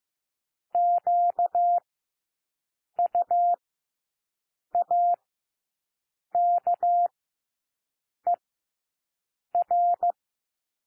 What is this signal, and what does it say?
Morse code, heard as a keyed tone, about 15 wpm, spaced as for 6 wpm, 700 Hz. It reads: QUAKER
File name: cwmp3 / W1 SOUND WORDS 15 wpm